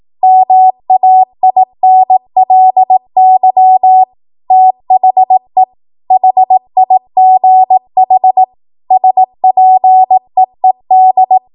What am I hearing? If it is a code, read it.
Morse code, heard as a keyed tone, about 18 wpm, 750 Hz. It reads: MAINLY THE HIGH SPEED
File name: Morse / 18WPM